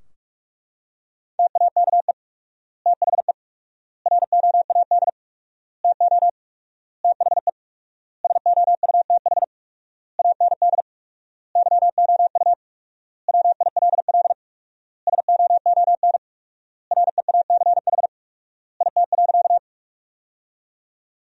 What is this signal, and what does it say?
Morse code, heard as a keyed tone, about 41 wpm, 700 Hz. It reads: TAKE THE ROAD TO THE SOUTH AND YOU WILL SOON REACH IT.
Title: 4x Word-Spacing - Sentences from Top 400 Words - 45wpm